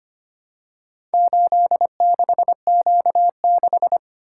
Morse code, heard as a keyed tone, about 25 wpm, 700 Hz. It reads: 86Q6
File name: practice / 25wpm / Four Letter-Number Combinations - 25wpm